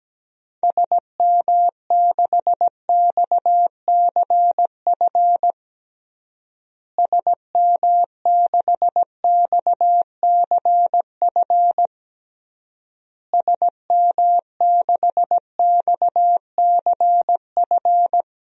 Morse code, heard as a keyed tone, about 17 wpm, 700 Hz. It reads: SM6XCF SM6XCF SM6XCF
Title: DXCC-100 Call Signs - 6 Characters - Repeated 3x - 17wpm